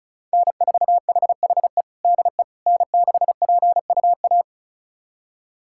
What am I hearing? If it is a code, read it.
Morse code, heard as a keyed tone, about 36 wpm, 700 Hz. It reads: N4HHE DE N6PUA
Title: USA Callsign DE Callsign - 5 Characters - 35wpm